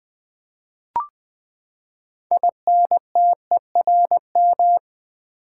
Morse code, heard as a keyed tone, about 20 wpm, 700 Hz. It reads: INTERM
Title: Build A Word - 6 Letters At A Time - 12-Letter Words - 20wpm